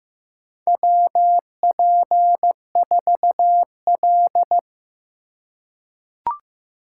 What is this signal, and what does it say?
Morse code, heard as a keyed tone, about 15 wpm, 700 Hz. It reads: WP4L
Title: USA Call Signs - 4 Characters - 4x Character Spacing - 15wpm